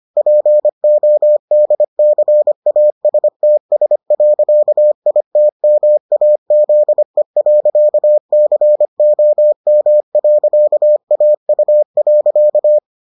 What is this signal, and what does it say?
Morse code, heard as a keyed tone, about 25 wpm, 600 Hz. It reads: PODCASTS.ITMAZE.COM.AU.